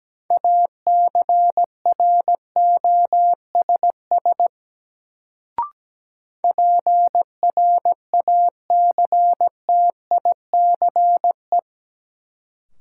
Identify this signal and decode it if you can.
Morse code, heard as a keyed tone, about 17 wpm, 700 Hz. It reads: ACROSS PRACTICE
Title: Sentences from Top 700 Words - New Word Drill - 17wpm